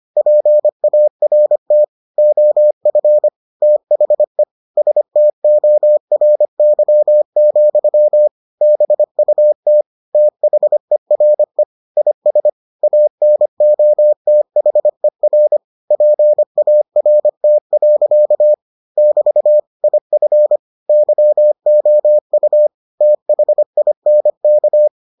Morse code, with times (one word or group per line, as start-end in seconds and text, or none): 0.2-1.8 PART
2.2-3.3 OF
3.6-4.4 THE
4.8-8.3 STORY,
8.6-9.8 BUT
10.1-11.6 THERE
12.0-12.5 IS
12.8-15.6 ANOTHER
15.9-18.5 PART.
19.0-19.6 <BT>
19.8-20.6 IF
20.9-22.7 YOU
23.0-24.9 THINK